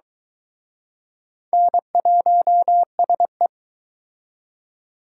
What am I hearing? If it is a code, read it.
Morse code, heard as a keyed tone, about 23 wpm, 700 Hz. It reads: N1SE